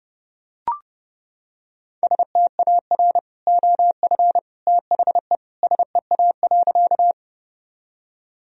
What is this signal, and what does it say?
Morse code, heard as a keyed tone, about 32 wpm, 700 Hz. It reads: STAR OF THE SEA.
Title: Sentences from Top 1000 Words - Review Part 2 - 30wpm